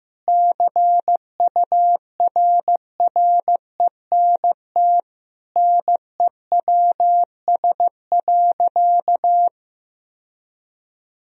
Morse code, heard as a keyed tone, about 15 wpm, 700 Hz. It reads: CURRENT NEWS.
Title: Sentences from Top 1000 Words - Review Part 4 - 15wpm